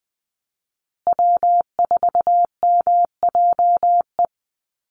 Morse code, heard as a keyed tone, about 20 wpm, 700 Hz. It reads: W4MJE
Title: Call Signs Repeated 2x - 20wpm